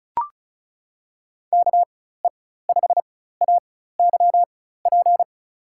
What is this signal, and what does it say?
Morse code, heard as a keyed tone, about 36 wpm, spaced as for 16 wpm, 700 Hz. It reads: KE5AYP
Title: USA Call Signs - 6 Characters - 4x Character Spacing - 35wpm